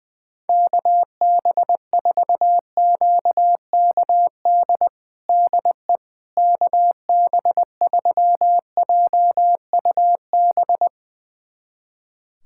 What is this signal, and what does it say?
Morse code, heard as a keyed tone, about 20 wpm, 700 Hz. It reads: KB4QKD DE KB3JUB
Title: USA Callsign DE Callsign - 6 Characters - 20wpm